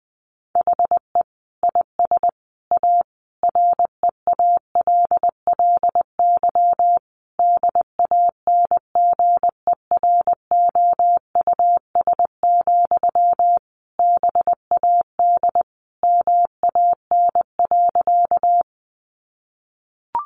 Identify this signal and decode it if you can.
Morse code, heard as a keyed tone, about 20 wpm, 700 Hz. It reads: HE IS A REALLY DANGEROUS, BAD MAN.
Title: Sentences from Top 1200 Words 20wpm